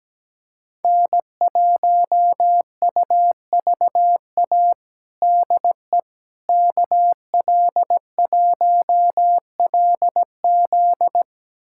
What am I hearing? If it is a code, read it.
Morse code, heard as a keyed tone, about 17 wpm, 700 Hz. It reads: N1UVA DE KL1LZ
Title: USA Callsign DE Callsign - 5 Characters - 17wpm